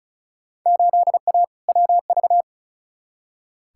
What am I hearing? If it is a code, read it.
Morse code, heard as a keyed tone, about 36 wpm, 700 Hz. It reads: 8A WV
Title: Field Day Exchanges - 35wpm